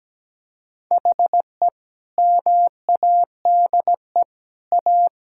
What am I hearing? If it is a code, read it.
Morse code, heard as a keyed tone, about 17 wpm, 700 Hz. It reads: HE MADE A
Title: Sentences from Top 1000 Words - Review Part 6 - 17wpm